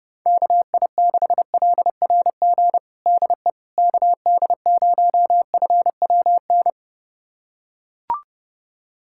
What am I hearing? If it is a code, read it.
Morse code, heard as a keyed tone, about 30 wpm, 700 Hz. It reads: KI6LRG DE KD0FWN
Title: USA Callsign DE Callsign - 6 Characters - 30wpm